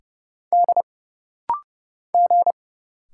Morse code, heard as a keyed tone, about 30 wpm, 700 Hz. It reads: D G
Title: Single Letters - Rapid-Fire 30wpm